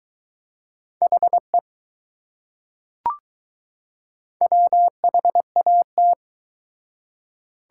Morse code, heard as a keyed tone, about 21 wpm, 700 Hz. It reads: HE WHAT